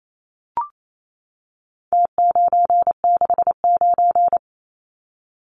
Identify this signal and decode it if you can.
Morse code, heard as a keyed tone, about 28 wpm, 700 Hz. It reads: T969
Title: Numbers - Contest Style - 4-Digit 28wpm